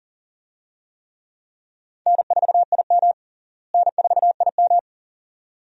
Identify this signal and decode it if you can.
Morse code, heard as a keyed tone, about 40 wpm, 700 Hz. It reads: N4IM N4IM
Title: USA Call Signs - 4 Characters - Repeated 2x - 40wpm